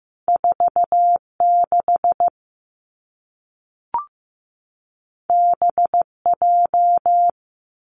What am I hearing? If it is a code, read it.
Morse code, heard as a keyed tone, about 15 wpm, 700 Hz. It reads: 4 6 B J
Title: Two Challenging Letter-Number Combinations 15wpm